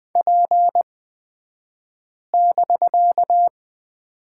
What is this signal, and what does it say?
Morse code, heard as a keyed tone, about 20 wpm, spaced as for 5 wpm, 700 Hz. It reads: P<BK>
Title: 249 Lesson 40d - BK - Two Characters 20wpm